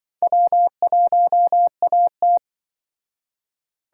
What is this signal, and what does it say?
Morse code, heard as a keyed tone, about 24 wpm, 700 Hz. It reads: W1AT
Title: Speed Racing - USA 1x2 Call Signs 20wpm